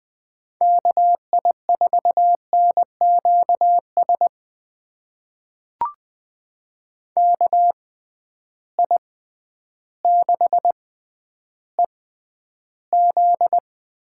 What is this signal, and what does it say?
Morse code, heard as a keyed tone, about 20 wpm, spaced as for 11 wpm, 700 Hz. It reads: KI4NQS KI6EZ
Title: USA Call Signs - 6 Characters - 6x Character Spacing - 20wpm